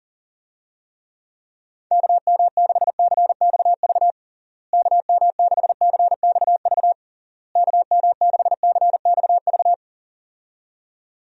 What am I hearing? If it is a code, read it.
Morse code, heard as a keyed tone, about 37 wpm, 700 Hz. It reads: KM6CXV KM6CXV KM6CXV
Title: USA Call Signs - 6 Characters - Repeated 3x - 40wpm